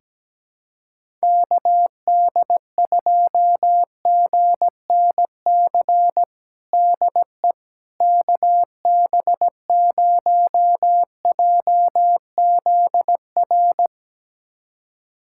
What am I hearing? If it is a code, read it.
Morse code, heard as a keyed tone, about 17 wpm, 700 Hz. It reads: KD2GNC DE KB0JZR
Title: USA Callsign DE Callsign - 6 Characters - 17wpm